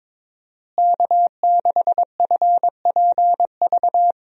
Morse code, heard as a keyed tone, about 22 wpm, 700 Hz. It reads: K6FPV
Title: USA Call Signs - 5 Characters - Repeated 3x - 22wpm